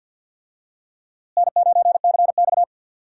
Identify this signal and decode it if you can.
Morse code, heard as a keyed tone, about 51 wpm, 700 Hz. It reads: N9CX
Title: USA Call Signs - 4 Characters - Repeated 2x - 50wpm